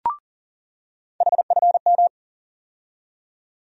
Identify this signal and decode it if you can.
Morse code, heard as a keyed tone, about 40 wpm, 700 Hz. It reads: HFM
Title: USA Call Sign Postfixes - 3 Letters 40wpm